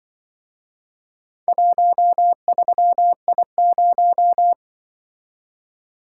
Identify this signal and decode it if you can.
Morse code, heard as a keyed tone, about 24 wpm, 700 Hz. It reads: 13I0
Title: Speed-Racing - Four Letter-Number Combinations - 20wpm